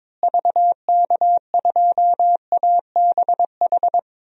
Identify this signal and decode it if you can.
Morse code, heard as a keyed tone, about 22 wpm, 700 Hz. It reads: VK2ABH
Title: DXCC-100 Call Signs - 6 Characters - Repeated 2x - 22wpm